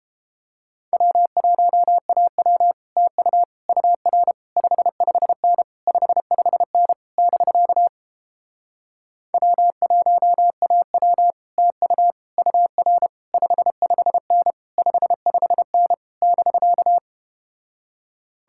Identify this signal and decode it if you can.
Morse code, heard as a keyed tone, about 32 wpm, 700 Hz. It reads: W1AW TU UR 55N 55N <BK> W1AW TU UR 55N 55N <BK>